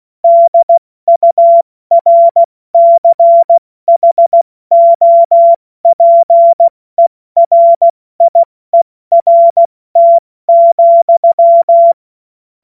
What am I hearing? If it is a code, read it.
Morse code, heard as a keyed tone, about 16 wpm, 680 Hz. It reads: DURCHOPERIERT,
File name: Morsekurs / Witze / Witz 02 April 2024 70 F80